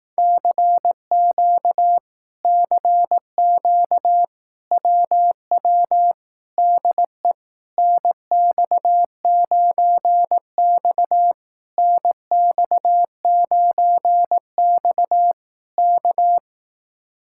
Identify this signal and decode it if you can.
Morse code, heard as a keyed tone, about 18 wpm, 700 Hz. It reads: CQ CQ WW DE NX9X NX9X K
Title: Speed-Racing Common QSO Phrases Encore - 15wpm